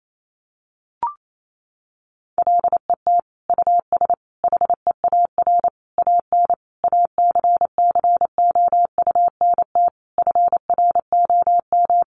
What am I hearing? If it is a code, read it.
Morse code, heard as a keyed tone, about 28 wpm, 700 Hz. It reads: LET US HEAR AN ACCOUNT FROM
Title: Sentences from Top 900 Words 28wpm